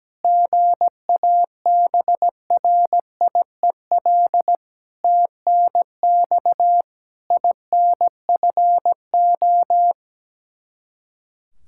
Morse code, heard as a keyed tone, about 17 wpm, 700 Hz. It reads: GABRIEL TNX INFO
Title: Practice QSO - Exchange 3 - More Info - 17wpm